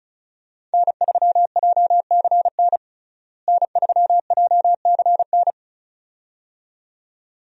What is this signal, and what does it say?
Morse code, heard as a keyed tone, about 35 wpm, 700 Hz. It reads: N3JCN N3JCN
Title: USA Call Signs - 5 Characters - Repeated 2x - 35wpm